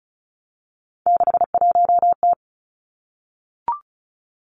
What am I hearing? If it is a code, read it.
Morse code, heard as a keyed tone, about 36 wpm, 700 Hz. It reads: 61T
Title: Three Challenging Letter-Number Combinations 35wpm